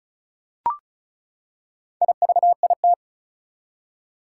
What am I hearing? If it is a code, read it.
Morse code, heard as a keyed tone, about 34 wpm, 700 Hz. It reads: IVIT
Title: Four Challenging Letter-Number Combinations 35wpm